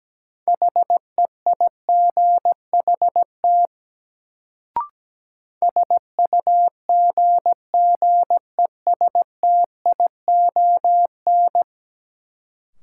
Morse code, heard as a keyed tone, about 17 wpm, 700 Hz. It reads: HEIGHT SUGGESTION
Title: Sentences from Top 1600 Words - New Word Drill Alt - 17wpm